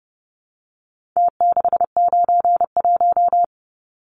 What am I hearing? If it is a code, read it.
Morse code, heard as a keyed tone, about 30 wpm, 700 Hz. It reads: T691
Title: Numbers - Contest Style - 4-Digit 30wpm